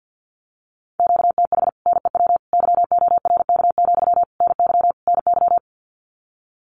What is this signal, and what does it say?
Morse code, heard as a keyed tone, about 50 wpm, 700 Hz. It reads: QTH NEW YORK, NY NY
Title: Common QSO Phrases Encore 50wpm